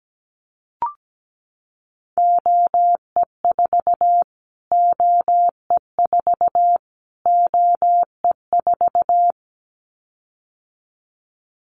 Morse code, heard as a keyed tone, about 17 wpm, 700 Hz. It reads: OE4 OE4 OE4
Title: Callsign Prefixes for Top 25 DXCC - Repeated 3x - 17wpm